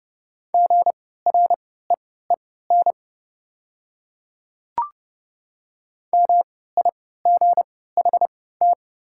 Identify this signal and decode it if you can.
Morse code, heard as a keyed tone, about 30 wpm, spaced as for 17 wpm, 700 Hz. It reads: GREEN MIGHT
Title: 3x Character Spacing - Top 500 Words - 30wpm